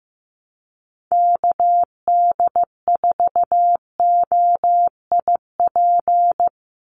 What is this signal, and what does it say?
Morse code, heard as a keyed tone, about 15 wpm, 700 Hz. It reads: KD4OIP
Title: USA Call Signs 15wpm